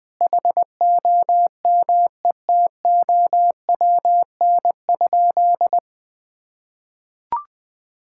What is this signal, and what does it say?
Morse code, heard as a keyed tone, about 20 wpm, 700 Hz. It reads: HOMETOWN?
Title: Sentences with 8-Word Phrases - Spoken Beginning - 20wpm